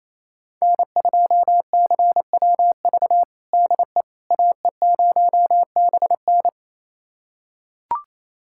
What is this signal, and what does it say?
Morse code, heard as a keyed tone, about 28 wpm, 700 Hz. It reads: N2CWV DE AE0BN
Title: USA Callsign DE Callsign - 5 Characters - 28wpm